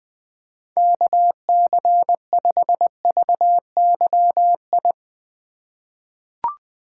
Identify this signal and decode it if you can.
Morse code, heard as a keyed tone, about 20 wpm, 700 Hz. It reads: KC5VYI
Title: USA Call Signs - 6 Characters - 3x Character Spacing - 20wpm